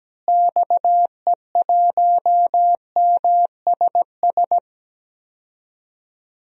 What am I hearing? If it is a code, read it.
Morse code, heard as a keyed tone, about 17 wpm, 700 Hz. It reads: XE1MSS
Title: DXCC-100 Call Signs - 6 Characters - Repeated 3x - 17wpm